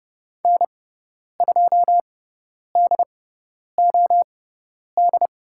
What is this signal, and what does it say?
Morse code, heard as a keyed tone, about 30 wpm, spaced as for 10 wpm, 700 Hz. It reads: N2DOD
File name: Lesson 17g - Letter u - Call Signs 30wpm